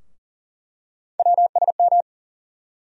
Morse code, heard as a keyed tone, about 40 wpm, 700 Hz. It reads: WSM
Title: USA Call Sign Postfixes - 3 Letters 40wpm